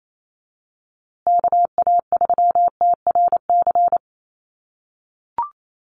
Morse code, heard as a keyed tone, about 28 wpm, 700 Hz. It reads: KA3TRC
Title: USA Call Signs Repeated 3x 28wpm